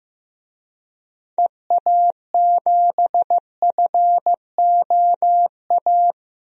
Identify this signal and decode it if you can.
Morse code, heard as a keyed tone, about 15 wpm, 700 Hz. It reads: EA7FOA
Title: DXCC-100 Call Signs - 6 Characters - 15wpm